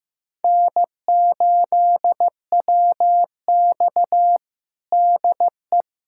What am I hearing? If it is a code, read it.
Morse code, heard as a keyed tone, about 16 wpm, 700 Hz. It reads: N8WX DE
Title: USA Callsign DE Callsign - 4 Characters - 15wpm